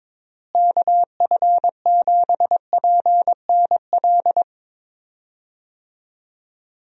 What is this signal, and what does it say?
Morse code, heard as a keyed tone, about 22 wpm, 700 Hz. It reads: KF7PNL
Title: USA Call Signs - 6 Characters - Repeated 2x - 22wpm